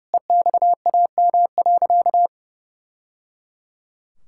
Morse code, heard as a keyed tone, about 30 wpm, 700 Hz. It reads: EXAM.
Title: Sentences from Top 1000 Words - Review Part 5 - 30wpm